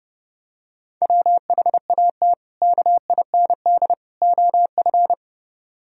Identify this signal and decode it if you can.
Morse code, heard as a keyed tone, about 30 wpm, 700 Hz. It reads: WHAT KIND OF